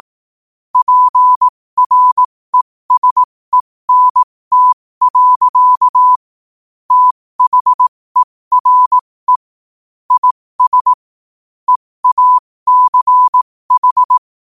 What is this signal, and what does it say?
Morse code, heard as a keyed tone, about 18 wpm, spaced as for 15 wpm, 1000 Hz. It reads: PRESENT. THERE IS EACH
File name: qotdfiles / 15wpm